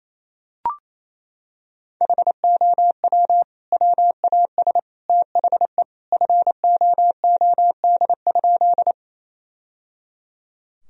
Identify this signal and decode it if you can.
Morse code, heard as a keyed tone, about 28 wpm, 700 Hz. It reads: HOW WAS THE FOOD?
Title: Sentences from Top 1000 Words - Review Part 1 - 28wpm